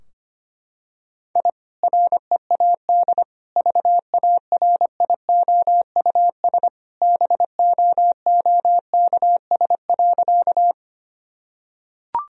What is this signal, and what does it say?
Morse code, heard as a keyed tone, about 25 wpm, 700 Hz. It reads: I READ VARIOUS BOOKS.
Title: Sentences from Top 1000 Words - Review Part 4 - 25wpm